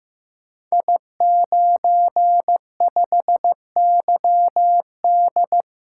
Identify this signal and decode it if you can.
Morse code, heard as a keyed tone, about 15 wpm, 700 Hz. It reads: I95YD
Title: Code Groups - 5 Letter:Numbers - 15wpm